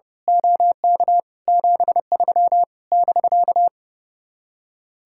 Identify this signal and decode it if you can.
Morse code, heard as a keyed tone, about 28 wpm, 700 Hz. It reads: OK 73 <BK>
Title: Speed-Racing Common QSO Phrases Encore - 30wpm